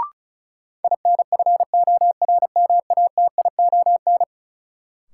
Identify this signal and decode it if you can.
Morse code, heard as a keyed tone, about 36 wpm, 700 Hz. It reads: INFORMATION
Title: Sentences from Top 600 Words - New Word Drill - 35wpm